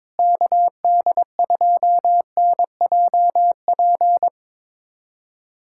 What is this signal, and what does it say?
Morse code, heard as a keyed tone, about 22 wpm, 700 Hz. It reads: KD2NJP
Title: USA Callsign DE Callsign - 6 Characters - 22wpm